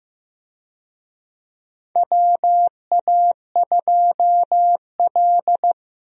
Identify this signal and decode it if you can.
Morse code, heard as a keyed tone, about 15 wpm, 700 Hz. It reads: WA2L
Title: USA Call Signs - 4 Characters - Repeated 2x - 15wpm